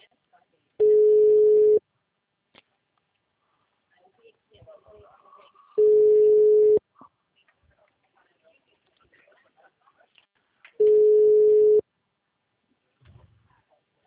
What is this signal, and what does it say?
Ringback tone (Europe, 425 Hz), with speech.